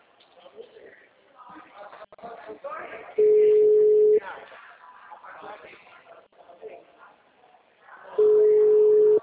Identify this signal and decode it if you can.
Ringback tone (Europe, 425 Hz), with speech.